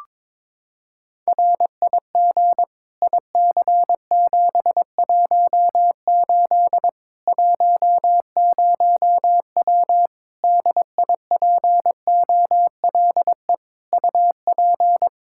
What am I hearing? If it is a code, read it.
Morse code, heard as a keyed tone, about 21 wpm, 700 Hz. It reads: RIG IC718 10W DIPOLE UP
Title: Practice QSO - Exchange 2 - Info - 22wpm